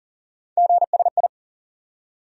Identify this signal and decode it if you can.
Morse code, heard as a keyed tone, about 40 wpm, 700 Hz. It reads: GSI
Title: USA Call Sign Postfixes - 3 Letters 40wpm